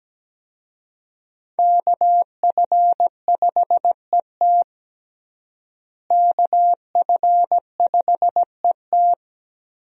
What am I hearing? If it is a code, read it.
Morse code, heard as a keyed tone, about 17 wpm, 700 Hz. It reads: KF5ET KF5ET
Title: USA Call Signs - 5 Characters - Repeated 2x - 17wpm